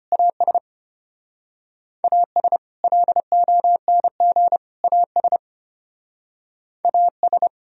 Morse code, heard as a keyed tone, about 32 wpm, 700 Hz. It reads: AS AS LONG AS AS